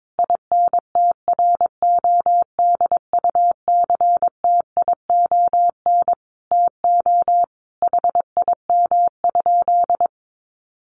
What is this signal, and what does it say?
Morse code, heard as a keyed tone, about 22 wpm, 700 Hz. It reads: INTRODUCTION TO HIM?
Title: Sentences from Top 1000 Words 22wpm